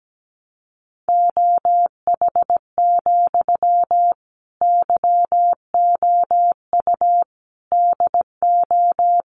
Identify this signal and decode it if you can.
Morse code, heard as a keyed tone, about 17 wpm, 700 Hz. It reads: OH, YOU DO
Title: Sentences from Top 600 Words 17wpm